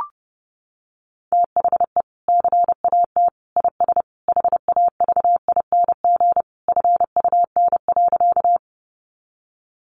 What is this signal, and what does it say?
Morse code, heard as a keyed tone, about 30 wpm, 700 Hz. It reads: THE CAT IS HAVING FUN.
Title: Easy Phrases 30wpm